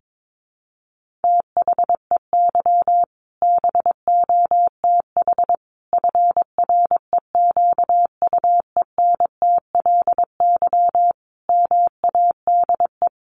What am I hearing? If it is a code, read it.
Morse code, heard as a keyed tone, about 22 wpm, 700 Hz. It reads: THEY BOTH FREQUENTLY MADE
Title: Sentences from Top 1300 Words 22wpm